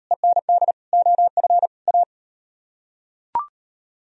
Morse code, heard as a keyed tone, about 34 wpm, 700 Hz. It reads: END OF A